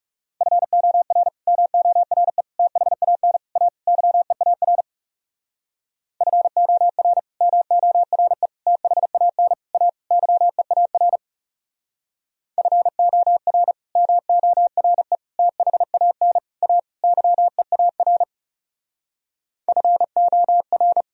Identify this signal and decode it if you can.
Morse code, heard as a keyed tone, about 37 wpm, 700 Hz. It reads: FOR MORE THAN A YEAR FOR MORE THAN A YEAR FOR MORE THAN A YEAR FOR
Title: Speed-Racing - Sets of 5 Words 30wpm